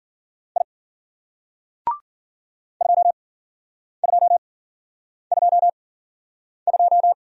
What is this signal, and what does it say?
Morse code, heard as a keyed tone, about 49 wpm, spaced as for 9 wpm, 700 Hz. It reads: I 2222
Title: Speed-Racing Single Letter-Number 40wpm